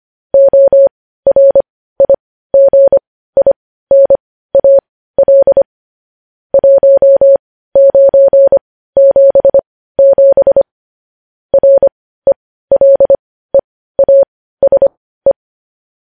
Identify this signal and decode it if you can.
Morse code, heard as a keyed tone, about 25 wpm, spaced as for 15 wpm, 550 Hz. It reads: ORIGINAL 1977 RELEASE
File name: podcast / News headlines-1x-25x15